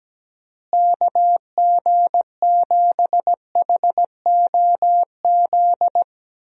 Morse code, heard as a keyed tone, about 17 wpm, 700 Hz. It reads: KG7HOZ
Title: USA Call Signs - 6 Characters - Repeated 3x - 17wpm